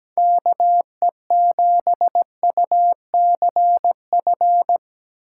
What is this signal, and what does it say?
Morse code, heard as a keyed tone, about 17 wpm, 700 Hz. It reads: KE7UCF
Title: USA Call Signs - 6 Characters - 2x Character Spacing - 17wpm